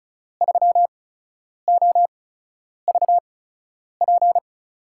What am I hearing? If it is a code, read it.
Morse code, heard as a keyed tone, about 35 wpm, spaced as for 10 wpm, 690 Hz. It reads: 3OVP
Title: USA Call Signs - 5 Characters - 8x Character Spacing - 35wpm